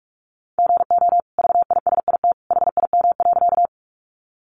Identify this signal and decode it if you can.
Morse code, heard as a keyed tone, about 44 wpm, 700 Hz. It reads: GO VISIT HIM.
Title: Sentences from Top 900 Words 45wpm